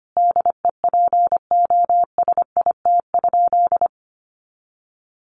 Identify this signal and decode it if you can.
Morse code, heard as a keyed tone, about 25 wpm, 700 Hz. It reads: DEPOSIT?
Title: Sentences from Top 1300 Words 25wpm